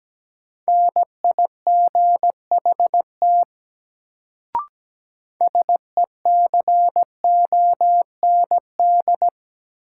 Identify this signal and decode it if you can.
Morse code, heard as a keyed tone, about 17 wpm, 700 Hz. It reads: NIGHT SECOND
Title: Sentences from Top 300 Words - New Word Drill - 17wpm